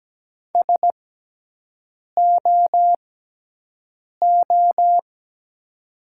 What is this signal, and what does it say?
Morse code, heard as a keyed tone, about 17 wpm, spaced as for 6 wpm, 700 Hz. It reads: SOO
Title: 6x Character Spacing - Top 500 Words - 17wpm